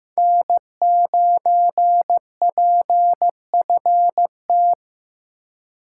Morse code, heard as a keyed tone, about 15 wpm, 700 Hz. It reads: N9PFT
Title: USA Call Signs - 3x Character Spacing - 15wpm